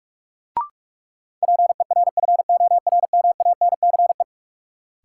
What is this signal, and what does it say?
Morse code, heard as a keyed tone, about 47 wpm, 700 Hz. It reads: PERFORMANCE
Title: Sentences from Top 1000 Words - New Word Drill - 45wpm